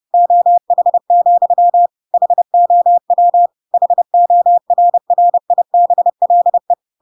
Morse code, heard as a keyed tone, about 30 wpm, 700 Hz. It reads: OH, HOW HORRIBLE
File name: Pinocchio / Pinocchio ch5 30 30 ik1hge